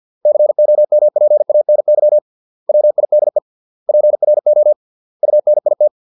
Morse code, heard as a keyed tone, about 50 wpm, 600 Hz. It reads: COMPANY WIDE PRO UNIT